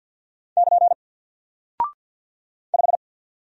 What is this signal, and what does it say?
Morse code, heard as a keyed tone, about 48 wpm, 700 Hz. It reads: <KN> 5
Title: Single Character (Letter:Number:Punctuation:Prosign) 50wpm